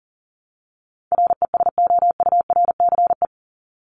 Morse code, heard as a keyed tone, about 40 wpm, 700 Hz. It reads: RESOURCE
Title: Top 1001-1250 Words 40wpm